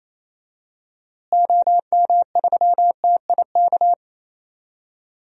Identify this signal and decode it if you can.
Morse code, heard as a keyed tone, about 28 wpm, 700 Hz. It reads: OM3TIK
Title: DXCC-100 Call Signs - 6 Characters - Repeated 3x - 28wpm